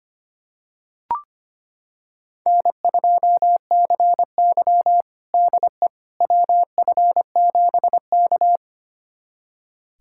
Morse code, heard as a keyed tone, about 25 wpm, 700 Hz. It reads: N2CY DE WF7K
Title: USA Callsign DE Callsign - 4 Characters - 25wpm